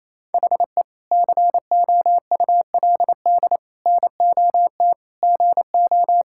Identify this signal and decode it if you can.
Morse code, heard as a keyed tone, about 28 wpm, 700 Hz. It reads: HE COULD NOT GO
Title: Sentences from Top 1000 Words - Review Part 6 - 28wpm